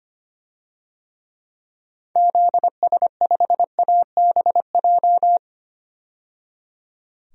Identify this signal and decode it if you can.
Morse code, heard as a keyed tone, about 25 wpm, 700 Hz. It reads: ZS5ABJ
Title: DXCC-100 Call Signs - 6 Characters - 25wpm